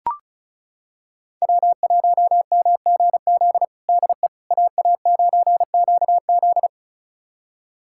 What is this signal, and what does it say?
Morse code, heard as a keyed tone, about 36 wpm, 700 Hz. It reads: W1MGZ DE AA9QZ
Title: USA Callsign DE Callsign - 5 Characters - 35wpm